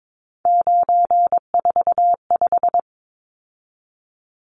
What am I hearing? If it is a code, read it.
Morse code, heard as a keyed tone, about 22 wpm, 700 Hz. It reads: 945
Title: Numbers - 3-Digit 22wpm